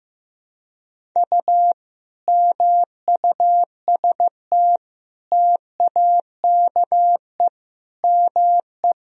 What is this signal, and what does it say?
Morse code, heard as a keyed tone, about 15 wpm, 700 Hz. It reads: U MUST TAKE ME
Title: Sentences from Top 200 Words and Abbreviations 15wpm